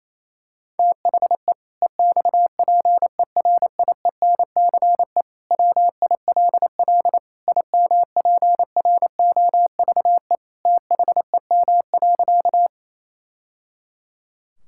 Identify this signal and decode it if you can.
Morse code, heard as a keyed tone, about 28 wpm, 700 Hz. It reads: THE EXPERIENCE WILL IMPROVE THEM.
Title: Sentences from Top 1000 Words - Review Part 5 - 28wpm